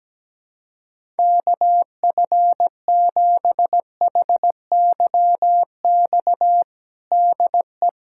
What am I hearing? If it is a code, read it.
Morse code, heard as a keyed tone, about 17 wpm, 700 Hz. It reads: KF7HYX DE
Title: USA Callsign DE Callsign - 6 Characters - 17wpm